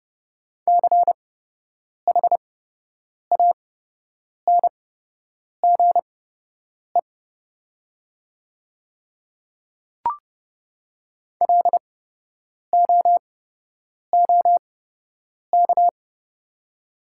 Morse code, heard as a keyed tone, about 30 wpm, spaced as for 8 wpm, 700 Hz. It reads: CHANGE LOOK
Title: 8x Character Spacing - Top 300 Words - 30wpm